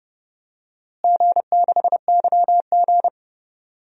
Morse code, heard as a keyed tone, about 30 wpm, 700 Hz. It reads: G6YG
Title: DXCC-100 Call Signs - 4 Characters - Repeated 2x - 30wpm